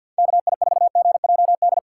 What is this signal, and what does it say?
Morse code, heard as a keyed tone, about 53 wpm, 700 Hz. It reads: KI4GJD
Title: USA Call Signs - 6 Characters - 4x Character Spacing - 50wpm